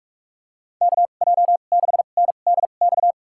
Morse code, heard as a keyed tone, about 46 wpm, spaced as for 33 wpm, 700 Hz. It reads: KJ6NDX
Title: USA Call Signs - 6 Characters - 2x Character Spacing - 45wpm